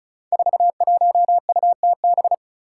Morse code, heard as a keyed tone, about 36 wpm, 700 Hz. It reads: 41UTB
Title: Five Challenging Letter-Number Combinations 35wpm